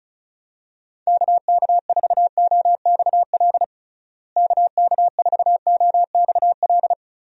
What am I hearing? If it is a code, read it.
Morse code, heard as a keyed tone, about 36 wpm, 700 Hz. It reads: KK4OXL KK4OXL
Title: USA Call Signs - 6 Characters - Repeated 2x - 35wpm